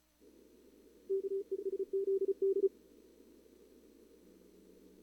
Morse code, heard as a keyed tone, about 33 wpm, 390 Hz. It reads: KESZD
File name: CWSprint / Feb2022 / k5tr-feb-na-sprint-cw-k5ot -0000